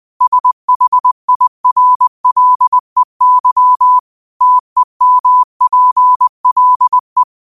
Morse code, heard as a keyed tone, about 20 wpm, 1000 Hz. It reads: SHIRLEY TEMPLE
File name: qotdfiles / 20wpm